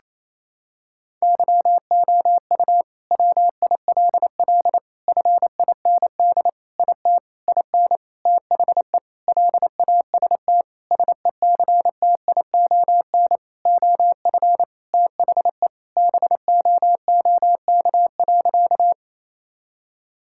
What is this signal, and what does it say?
Morse code, heard as a keyed tone, about 28 wpm, 700 Hz. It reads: YOU WILL FIND IT IN THE LAST SECTION OF THE BOOK.
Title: Sentences from Top 1000 Words - Review Part 5 - 28wpm